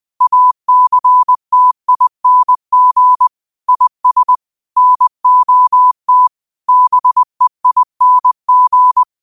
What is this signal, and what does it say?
Morse code, heard as a keyed tone, about 20 wpm, 1000 Hz. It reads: ACTING IS NOT BEING